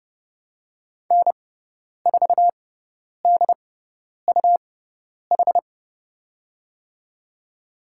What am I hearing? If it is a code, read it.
Morse code, heard as a keyed tone, about 31 wpm, spaced as for 10 wpm, 700 Hz. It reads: N4DUH
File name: Lesson 17g - Letter u - Call Signs 30wpm